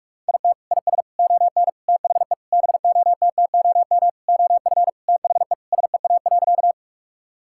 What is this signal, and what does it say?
Morse code, heard as a keyed tone, about 46 wpm, 700 Hz. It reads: IT IS ON THE BOTTOM OF THE SEA.
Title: Sentences from Top 1000 Words - Review Part 6 - 45wpm